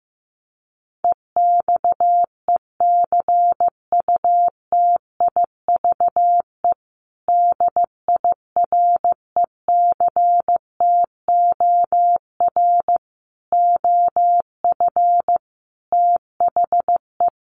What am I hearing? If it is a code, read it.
Morse code, heard as a keyed tone, about 15 wpm, 700 Hz. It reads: EXECUTIVE DIRECTOR OF THE
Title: Sets of 5 Words Encore2 15wpm